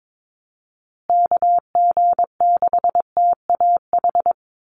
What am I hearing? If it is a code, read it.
Morse code, heard as a keyed tone, about 22 wpm, 700 Hz. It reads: KG6TAH
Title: USA Call Signs Repeated 3x 22wpm